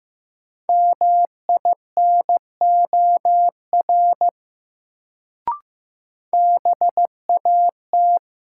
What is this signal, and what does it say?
Morse code, heard as a keyed tone, about 15 wpm, 700 Hz. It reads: MINOR BAT
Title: Sentences from Top 1500 Words - New Word Drill Alt - 15wpm